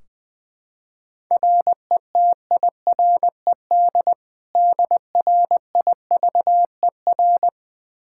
Morse code, heard as a keyed tone, about 20 wpm, 700 Hz. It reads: RETIRED DRIVER
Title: Practice QSO - Exchange 3 - More Info - 20wpm